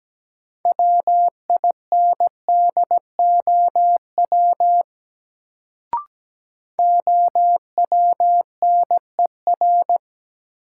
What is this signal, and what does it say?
Morse code, heard as a keyed tone, about 17 wpm, 700 Hz. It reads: WINDOW OWNER
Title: Sentences from Top 1600 Words - New Word Drill - 17wpm